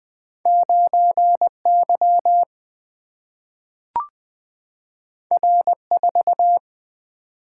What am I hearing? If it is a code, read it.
Morse code, heard as a keyed tone, about 20 wpm, 700 Hz. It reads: 9 Y R 4
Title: Single-Letter Single-Number 20wpm